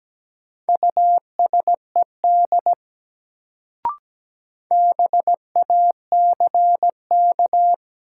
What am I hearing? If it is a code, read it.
Morse code, heard as a keyed tone, about 17 wpm, 700 Hz. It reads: USED BACK
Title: Sets of 2 Words - New Word Drill - 17wpm